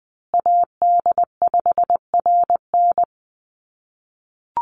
Morse code, heard as a keyed tone, about 20 wpm, 700 Hz. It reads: AD5RN
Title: USA Call Signs Repeated 2x 20wpm